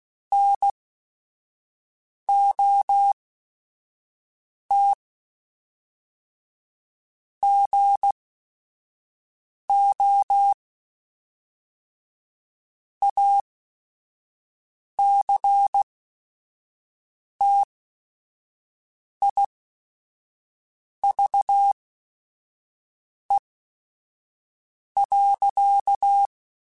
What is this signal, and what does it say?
Morse code, heard as a keyed tone, about 16 wpm, spaced as for 5 wpm, 800 Hz. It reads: NOT GO ACTIVE.